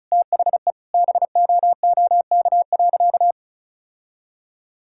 Morse code, heard as a keyed tone, about 34 wpm, 700 Hz. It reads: THE BOOK.
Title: Sentences from Top 1000 Words - Review Part 6 - 35wpm